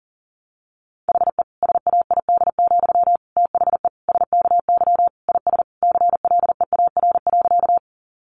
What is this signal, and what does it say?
Morse code, heard as a keyed tone, about 40 wpm, 700 Hz. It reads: HE SAID, THE SKY IS CLEAR.
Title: Sentences from Top 1200 Words 40wpm